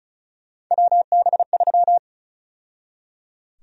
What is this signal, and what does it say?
Morse code, heard as a keyed tone, about 34 wpm, 700 Hz. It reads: WB3
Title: USA Call Sign Prefixes - 35wpm